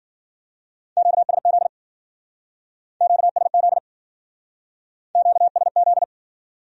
Morse code, heard as a keyed tone, about 49 wpm, 700 Hz. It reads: QSZ QSZ QSZ
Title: Speed-Racing Q-Codes - 40wpm